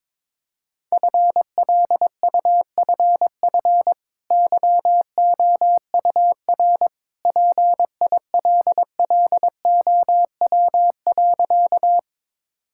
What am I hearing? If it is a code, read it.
Morse code, heard as a keyed tone, about 22 wpm, 690 Hz. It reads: FLUFF YOUR PILLOW.
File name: practice / 22wpm / Easy Sentences with Hints - 22wpm